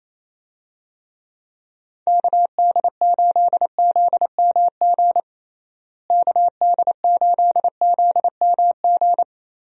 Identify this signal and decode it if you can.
Morse code, heard as a keyed tone, about 28 wpm, 700 Hz. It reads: KD8ZMG KD8ZMG
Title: USA Call Signs - 6 Characters - Repeated 2x - 28wpm